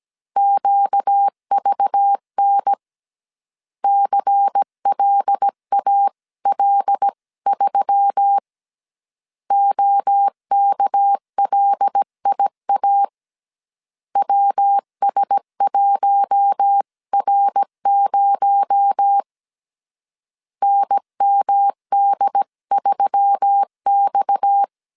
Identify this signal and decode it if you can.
Morse code, heard as a keyed tone, about 17 wpm, 800 Hz. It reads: QVN CLAL3 OKLIA WS1R0 NMD3X
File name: CW / DET / 1 6 1500C S A 16wpm 2635